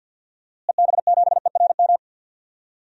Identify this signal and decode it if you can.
Morse code, heard as a keyed tone, about 46 wpm, 700 Hz. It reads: EB7ERM
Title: DXCC-100 Call Signs - 6 Characters - Repeated 2x - 50wpm